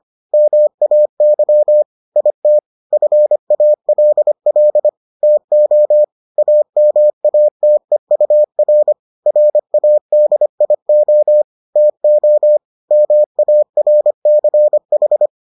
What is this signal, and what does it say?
Morse code, heard as a keyed tone, about 25 wpm, 600 Hz. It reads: MAY IT FALL TO AMATEUR RADIO TO MARCH